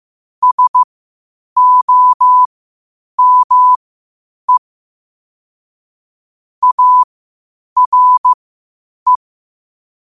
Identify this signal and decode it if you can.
Morse code, heard as a keyed tone, about 15 wpm, spaced as for 8 wpm, 1000 Hz. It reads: SOME ARE
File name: qotdfiles / 8wpm